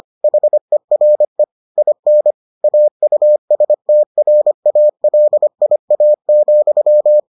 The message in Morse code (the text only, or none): HERE IN AUSTRALIA,